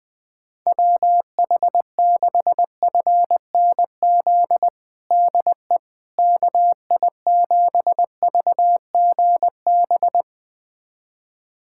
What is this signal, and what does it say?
Morse code, heard as a keyed tone, about 20 wpm, 700 Hz. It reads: WH6FNZ DE KI7VGB
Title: USA Callsign DE Callsign - 6 Characters - 20wpm